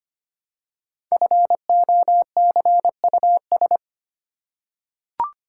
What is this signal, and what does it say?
Morse code, heard as a keyed tone, about 25 wpm, 700 Hz. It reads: FOCUS
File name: 5 Letter Words - First 3 Letters Given - 25wpm